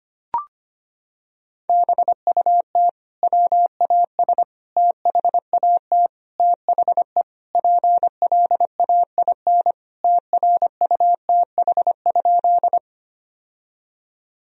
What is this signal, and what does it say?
Morse code, heard as a keyed tone, about 25 wpm, 700 Hz. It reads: BUT WAS THAT THE PLAIN TRUTH?
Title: Sentences from Top 1000 Words - Review Part 2 - 25wpm